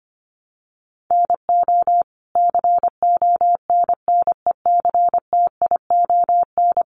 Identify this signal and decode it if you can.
Morse code, heard as a keyed tone, about 25 wpm, 700 Hz. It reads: NO CONNECTION
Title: Sentences from Top 1300 Words 25wpm